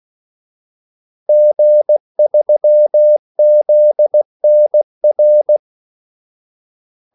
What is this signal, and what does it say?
Morse code, heard as a keyed tone, about 16 wpm, 600 Hz. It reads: G3ZNR
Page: Kaldesignaler 16 wpm